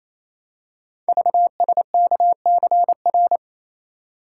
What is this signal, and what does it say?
Morse code, heard as a keyed tone, about 29 wpm, 700 Hz. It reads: VSKCR
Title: Code Groups - 5 Letters - 28wpm